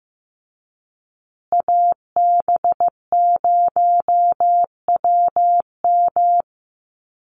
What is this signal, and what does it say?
Morse code, heard as a keyed tone, about 15 wpm, 700 Hz. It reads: AB0WM
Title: USA Call Signs Repeated 2x 15wpm